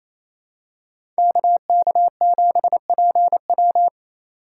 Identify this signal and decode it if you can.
Morse code, heard as a keyed tone, about 28 wpm, 700 Hz. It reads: KK7PW
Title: USA Call Signs - 5 Characters - 4x Character Spacing - 28wpm